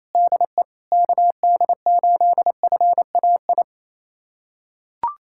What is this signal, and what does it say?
Morse code, heard as a keyed tone, about 28 wpm, 700 Hz. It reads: DE KD8FAI
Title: USA Callsign DE Callsign 28wpm